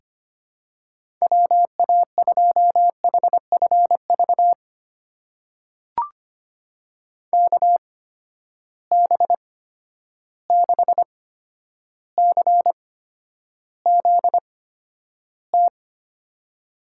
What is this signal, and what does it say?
Morse code, heard as a keyed tone, about 25 wpm, spaced as for 11 wpm, 700 Hz. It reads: WA2HFV KB6CZT
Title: USA Call Signs - 6 Characters - 8x Character Spacing - 25wpm